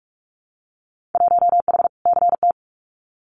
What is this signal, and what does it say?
Morse code, heard as a keyed tone, about 44 wpm, 700 Hz. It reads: 1H CT
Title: Winter Field Day Exchanges 45wpm